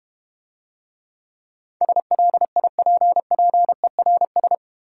Morse code, heard as a keyed tone, about 31 wpm, 700 Hz. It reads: SLIPPERS